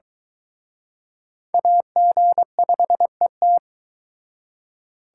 Morse code, heard as a keyed tone, about 23 wpm, 700 Hz. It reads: AG5ET